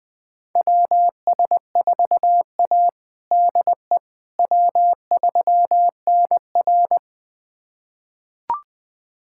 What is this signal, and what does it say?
Morse code, heard as a keyed tone, about 20 wpm, 700 Hz. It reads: WS4A DE W3NR
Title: USA Callsign DE Callsign - 4 Characters - 20wpm